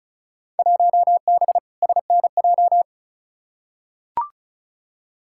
Morse code, heard as a keyed tone, about 36 wpm, 700 Hz. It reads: 1B SNJ
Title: Field Day Exchanges - 35wpm